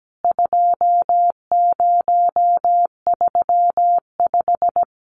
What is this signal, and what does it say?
Morse code, heard as a keyed tone, about 17 wpm, 700 Hz. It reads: 2035
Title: CWT Exchange 17wpm